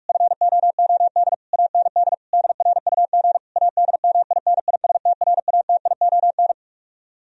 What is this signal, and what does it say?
Morse code, heard as a keyed tone, about 45 wpm, 700 Hz. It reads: FOOD AND DRUG ADMINISTRATION